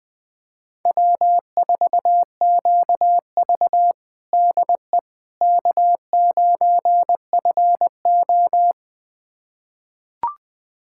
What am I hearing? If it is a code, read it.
Morse code, heard as a keyed tone, about 20 wpm, 700 Hz. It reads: W4QV DE K9FO
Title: USA Callsign DE Callsign - 4 Characters - 20wpm